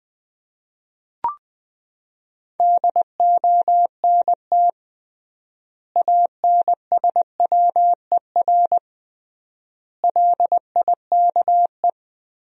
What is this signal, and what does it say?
Morse code, heard as a keyed tone, about 20 wpm, 700 Hz. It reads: DONT ANSWER LIKE
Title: 3x Word-Spacing - Sentences from Top 400 Words - 20wpm